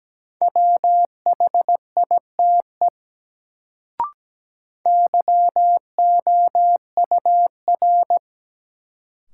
Morse code, heard as a keyed tone, about 17 wpm, 700 Hz. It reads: WHITE YOUR
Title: Sets of 2 Words - New Word Drill - 17wpm